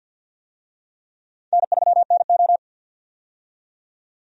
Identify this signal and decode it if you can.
Morse code, heard as a keyed tone, about 53 wpm, 700 Hz. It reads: N3NO
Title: USA Call Signs - 4 Characters - 50wpm